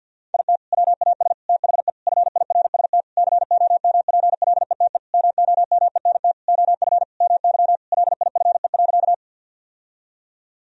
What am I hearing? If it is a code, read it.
Morse code, heard as a keyed tone, about 50 wpm, 700 Hz. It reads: IT WAS THE FIRST COMPLETE MOMENT OF MY LIFE.
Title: Sentences from Top 1000 Words - Review Part 2 - 50wpm